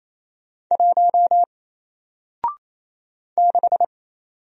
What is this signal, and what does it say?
Morse code, heard as a keyed tone, about 28 wpm, 700 Hz. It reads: 1 6
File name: Single-Letter-Number-28wpm